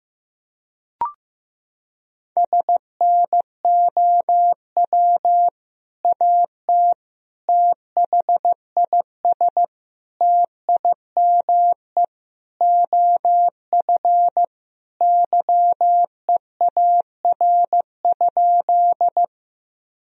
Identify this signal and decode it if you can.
Morse code, heard as a keyed tone, about 14 wpm, 700 Hz. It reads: SNOW AT THIS TIME OF YEAR?
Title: Sentences from Top 1000 Words - Review Part 3 - 15wpm